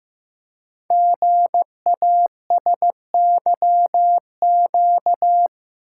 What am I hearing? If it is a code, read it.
Morse code, heard as a keyed tone, about 15 wpm, 700 Hz. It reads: GASYQ
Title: Code Groups - 5 Letter:Numbers - 15wpm